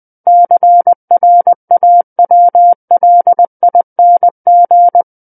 Morse code, heard as a keyed tone, about 20 wpm, 700 Hz. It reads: CRAWLING
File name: 20 WPM